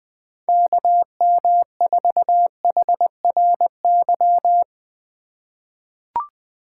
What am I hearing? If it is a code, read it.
Morse code, heard as a keyed tone, about 21 wpm, 700 Hz. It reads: KM4HRY
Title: USA Call Signs - 6 Characters - 4x Character Spacing - 20wpm